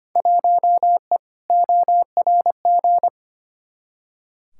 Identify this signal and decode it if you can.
Morse code, heard as a keyed tone, about 25 wpm, 700 Hz. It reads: 1E ORG
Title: Field Day Exchanges - 25wpm